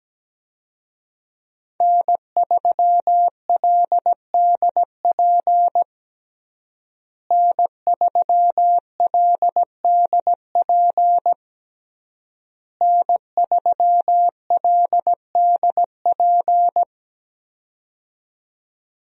Morse code, heard as a keyed tone, about 17 wpm, 700 Hz. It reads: N3LDP N3LDP N3LDP
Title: USA Call Signs - 5 Characters - Repeated 3x - 17wpm